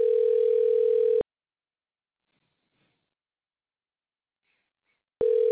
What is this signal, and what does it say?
Ringback tone (North America), 440 + 480 Hz.